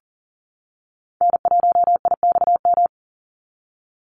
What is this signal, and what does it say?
Morse code, heard as a keyed tone, about 40 wpm, 700 Hz. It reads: N1IXM
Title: USA Call Signs 40wpm